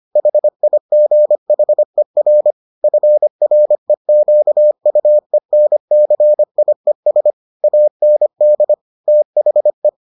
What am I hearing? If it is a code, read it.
Morse code, heard as a keyed tone, about 25 wpm, 600 Hz. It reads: HIGHER FREQUENCIES AND THE